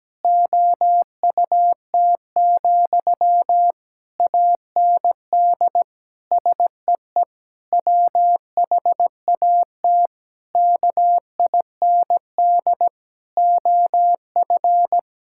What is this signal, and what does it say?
Morse code, heard as a keyed tone, about 17 wpm, 700 Hz. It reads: OUT, AND SEE WHAT KIND OF
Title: Sentences from Top 1000 Words - Review Part 6 - 17wpm